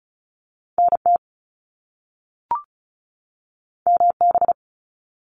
Morse code, heard as a keyed tone, about 36 wpm, 700 Hz. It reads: NT MB
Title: Canadian Provinces and Territories 35wpm